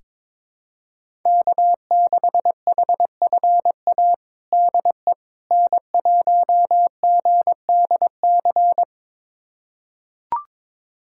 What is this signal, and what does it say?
Morse code, heard as a keyed tone, about 21 wpm, 700 Hz. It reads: K6HFA DE N1GDC
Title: USA Callsign DE Callsign - 5 Characters - 22wpm